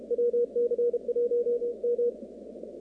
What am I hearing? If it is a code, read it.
Morse code, heard as a keyed tone, about 32 wpm, 480 Hz. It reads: WC1M
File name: contests / 2019-cw-wpx-kt5j-k5pi- -2330 0-38